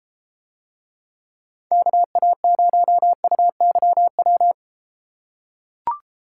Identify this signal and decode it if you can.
Morse code, heard as a keyed tone, about 30 wpm, 700 Hz. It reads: KA0UYW